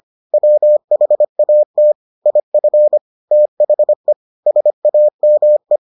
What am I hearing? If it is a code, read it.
Morse code, heard as a keyed tone, about 25 wpm, 600 Hz. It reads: WHAT IF THE SAME